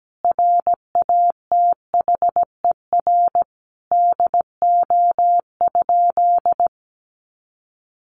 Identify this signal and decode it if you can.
Morse code, heard as a keyed tone, about 17 wpm, 700 Hz. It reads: RATHER DO?
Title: Sentences from Top 700 Words 17wpm